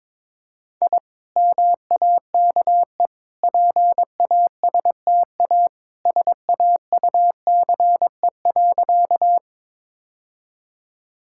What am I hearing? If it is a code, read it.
Morse code, heard as a keyed tone, about 22 wpm, 700 Hz. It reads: I MAKE PASTA SAUCE.
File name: practice / 22wpm / Easy Sentences with Hints - 22wpm